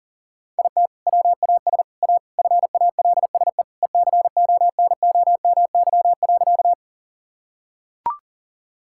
Morse code, heard as a keyed tone, about 40 wpm, 700 Hz. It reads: IT WAS A FALSE ECONOMY.
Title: Sentences from Top 1600 Words 40wpm